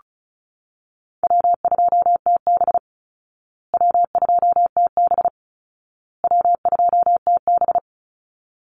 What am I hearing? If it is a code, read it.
Morse code, heard as a keyed tone, about 34 wpm, 700 Hz. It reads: W2TB W2TB W2TB
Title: Call Signs Repeated 2x - 35wpm